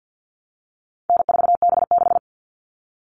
Morse code, heard as a keyed tone, about 50 wpm, 700 Hz. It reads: N4B6
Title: All Cut Numbers - 4-Digit 50wpm